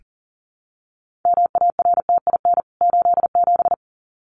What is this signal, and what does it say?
Morse code, heard as a keyed tone, about 40 wpm, 700 Hz. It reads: MARTIN 87
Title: CWT Exchange 40wpm